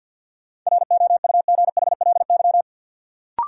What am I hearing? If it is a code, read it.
Morse code, heard as a keyed tone, about 51 wpm, 700 Hz. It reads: ROUGHLY
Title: Sentences from Top 1500 Words - New Word Drill Alt - 50wpm